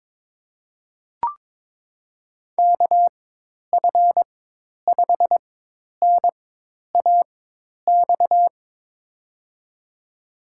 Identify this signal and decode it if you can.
Morse code, heard as a keyed tone, about 21 wpm, spaced as for 10 wpm, 700 Hz. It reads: KF5NAX
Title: USA Call Signs - 4x Character Spacing - 22wpm